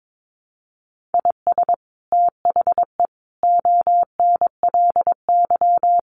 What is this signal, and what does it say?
Morse code, heard as a keyed tone, about 22 wpm, 700 Hz. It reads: IS THE ONLY
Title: Sets of 5 Words Encore 22wpm